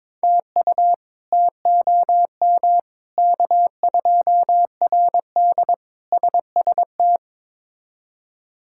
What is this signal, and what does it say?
Morse code, heard as a keyed tone, about 21 wpm, 700 Hz. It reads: TU TOM K2RD SST
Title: K1USN SST QSO Practice - 22wpm